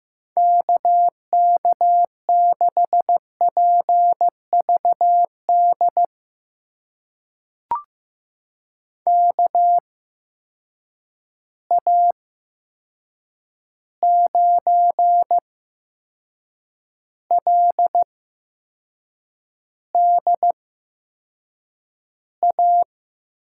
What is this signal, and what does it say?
Morse code, heard as a keyed tone, about 15 wpm, spaced as for 6 wpm, 700 Hz. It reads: KK6PVD KA9LDA
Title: USA Call Signs - 6 Characters - 8x Character Spacing - 15wpm